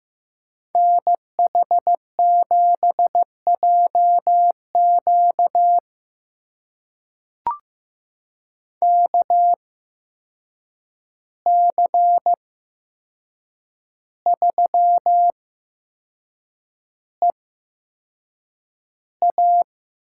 Morse code, heard as a keyed tone, about 15 wpm, spaced as for 6 wpm, 700 Hz. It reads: NH7JQ KC3EA
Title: USA Call Signs - 8x Character Spacing - 15wpm